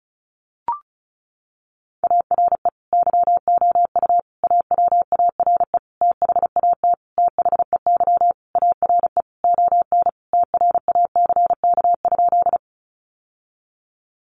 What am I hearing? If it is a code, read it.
Morse code, heard as a keyed tone, about 34 wpm, 700 Hz. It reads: ARE YOU AWARE THAT THEY ARE ON TRACK?
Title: Sentences from Top 1200 Words 35wpm